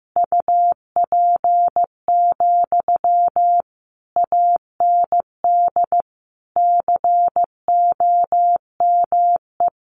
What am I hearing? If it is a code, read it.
Morse code, heard as a keyed tone, about 15 wpm, 700 Hz. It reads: UP, AND COME
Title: Sentences from Top 1000 Words 15wpm